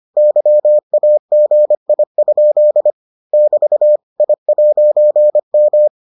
Morse code, heard as a keyed tone, about 25 wpm, 600 Hz. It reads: YAGI? <BT> I'M